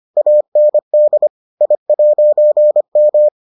Morse code, heard as a keyed tone, about 25 wpm, 600 Hz. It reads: AND I'M